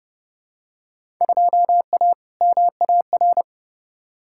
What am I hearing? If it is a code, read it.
Morse code, heard as a keyed tone, about 30 wpm, 700 Hz. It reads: 2A MAR
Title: Field Day Exchanges - 30wpm